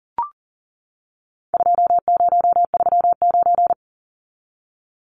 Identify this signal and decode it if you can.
Morse code, heard as a keyed tone, about 40 wpm, 700 Hz. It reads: 2039
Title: Numbers - 4-Digit 40wpm